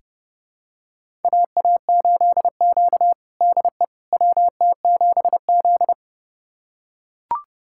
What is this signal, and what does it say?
Morse code, heard as a keyed tone, about 30 wpm, 700 Hz. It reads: AA8Q DE WT7Z
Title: USA Callsign DE Callsign - 4 Characters - 30wpm